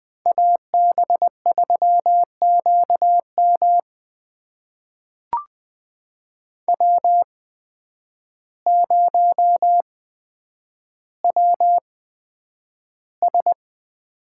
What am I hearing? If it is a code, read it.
Morse code, heard as a keyed tone, about 20 wpm, spaced as for 9 wpm, 700 Hz. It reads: AB3QM W0WS
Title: USA Call Signs - 5 Characters - 8x Character Spacing - 20wpm